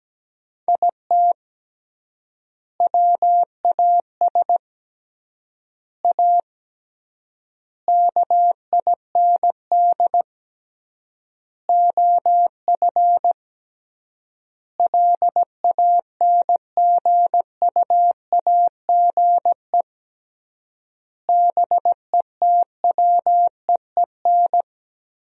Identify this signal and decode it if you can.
Morse code, heard as a keyed tone, about 17 wpm, 700 Hz. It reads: IT WAS A KIND OF LANGUAGE BETWEEN
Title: 3x Word-Spacing - Sentences from Top 500 Words - 17wpm